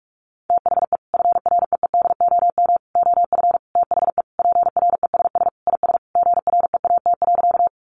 Morse code, heard as a keyed tone, about 44 wpm, 700 Hz. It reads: THE FREEDOM OF THE PRESS IS GREAT.
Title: Sentences from Top 800 Words 45wpm